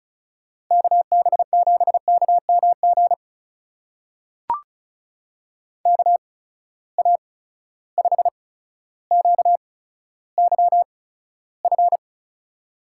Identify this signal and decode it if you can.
Morse code, heard as a keyed tone, about 36 wpm, spaced as for 15 wpm, 700 Hz. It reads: KB7KMG KA5QYF
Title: USA Call Signs - 6 Characters - 8x Character Spacing - 35wpm